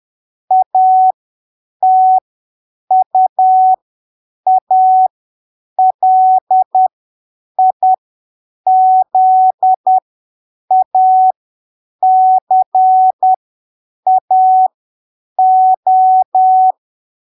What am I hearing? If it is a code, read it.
Morse code, heard as a keyed tone, about 10 wpm, spaced as for 7 wpm, 750 Hz. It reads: ATUALIZACAO
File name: arquivos / texto / texto 22 10ppm